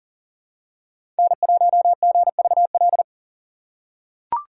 Morse code, heard as a keyed tone, about 40 wpm, 700 Hz. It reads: N1GVL
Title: USA Call Signs - 5 Characters - 4x Character Spacing - 40wpm